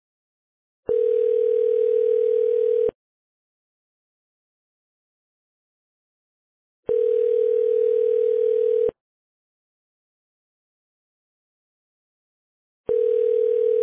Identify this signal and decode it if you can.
Ringback tone (North America), 440 + 480 Hz.